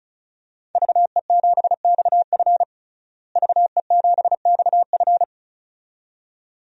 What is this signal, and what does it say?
Morse code, heard as a keyed tone, about 36 wpm, 700 Hz. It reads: VE7XF VE7XF
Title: DXCC-100 Call Signs - 5 Characters - Repeated 2x - 35wpm